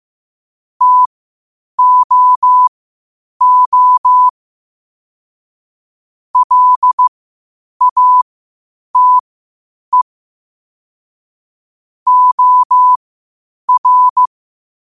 Morse code, heard as a keyed tone, about 15 wpm, spaced as for 8 wpm, 1000 Hz. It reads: TOO LATE OR
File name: 8wpm